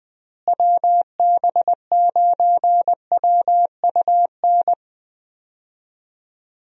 Morse code, heard as a keyed tone, about 20 wpm, 700 Hz. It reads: WB9WUN
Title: USA Call Signs - 6 Characters - Repeated 2x - 20wpm